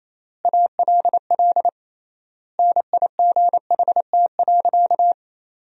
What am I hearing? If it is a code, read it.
Morse code, heard as a keyed tone, about 28 wpm, 700 Hz. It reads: ALL NIGHT.
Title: 3x Word-Spacing - Sentences from Top 500 Words - 28wpm